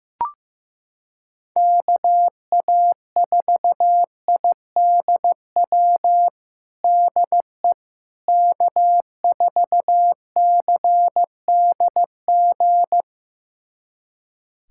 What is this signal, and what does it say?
Morse code, heard as a keyed tone, about 15 wpm, 700 Hz. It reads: KA4IDW DE K4CDG
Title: USA Callsign DE Callsign 15wpm